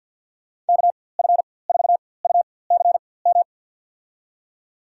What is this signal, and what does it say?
Morse code, heard as a keyed tone, about 52 wpm, spaced as for 23 wpm, 700 Hz. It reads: KF4UCM
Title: USA Call Signs - 6 Characters - 4x Character Spacing - 50wpm